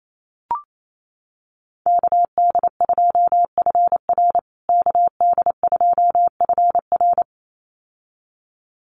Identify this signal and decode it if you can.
Morse code, heard as a keyed tone, about 28 wpm, 700 Hz. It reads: KD2FR KD2FR
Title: USA Call Signs Repeated 2x 28wpm